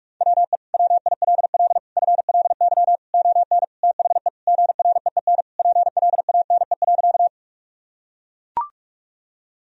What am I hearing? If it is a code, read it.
Morse code, heard as a keyed tone, about 44 wpm, 700 Hz. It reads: WE WILL FLY ON THE GREEN PLANE.
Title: Sentences from Top 1000 Words - Review Part 3 - 45wpm